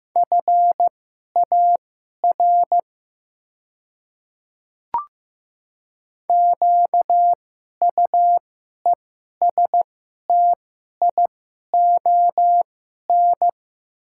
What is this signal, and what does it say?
Morse code, heard as a keyed tone, about 15 wpm, spaced as for 11 wpm, 700 Hz. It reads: FAR QUESTION
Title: 2x Character Spacing - Top 500 Words - 15wpm